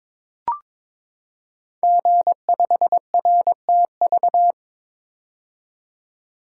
Morse code, heard as a keyed tone, about 22 wpm, 700 Hz. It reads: G5RTV
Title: Code Groups - 5 Letter:Numbers - 22wpm